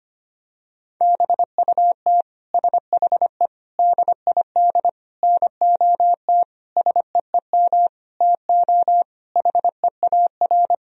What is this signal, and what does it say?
Morse code, heard as a keyed tone, about 25 wpm, 700 Hz. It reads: BUT SHE DID NOT SEEM TO HEAR
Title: Sentences from Top 1000 Words - Review Part 1 - 25wpm